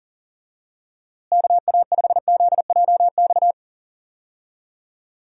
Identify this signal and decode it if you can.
Morse code, heard as a keyed tone, about 38 wpm, 700 Hz. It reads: KA5ZJX
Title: USA Call Signs - 6 Characters - Repeated 3x - 40wpm